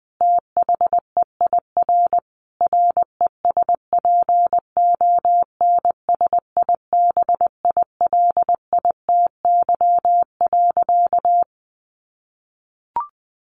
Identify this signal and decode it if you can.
Morse code, heard as a keyed tone, about 20 wpm, 700 Hz. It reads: THEIR RESPONSIBILITY.
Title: Sentences from Top 1500 Words 20wpm